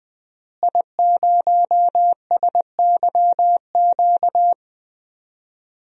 Morse code, heard as a keyed tone, about 21 wpm, 700 Hz. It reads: I0SYQ
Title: DXCC-100 Call Signs - 5 Characters - Repeated 3x - 20wpm